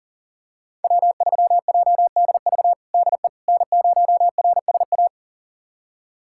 Morse code, heard as a keyed tone, about 40 wpm, 700 Hz. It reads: W3JDV DE N0RSA
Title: USA Callsign DE Callsign - 5 Characters - 40wpm